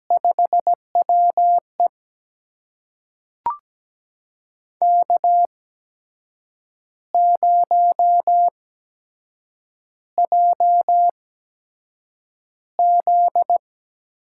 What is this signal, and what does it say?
Morse code, heard as a keyed tone, about 17 wpm, spaced as for 7 wpm, 700 Hz. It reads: 5WE K0JZ
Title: USA Call Signs - 4 Characters - 8x Character Spacing - 17wpm